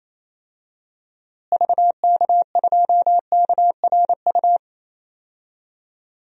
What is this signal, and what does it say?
Morse code, heard as a keyed tone, about 27 wpm, 700 Hz. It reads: VK2KRU
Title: DXCC-100 Call Signs - 6 Characters - Repeated 2x - 28wpm